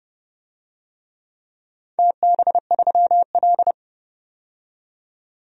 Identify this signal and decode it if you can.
Morse code, heard as a keyed tone, about 30 wpm, 700 Hz. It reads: TB3L
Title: DXCC-100 Call Signs - 4 Characters - 30wpm